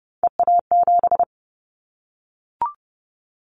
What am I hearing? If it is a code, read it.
Morse code, heard as a keyed tone, about 30 wpm, 700 Hz. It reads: EA7
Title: Callsign Prefixes for Top 25 DXCC - 30wpm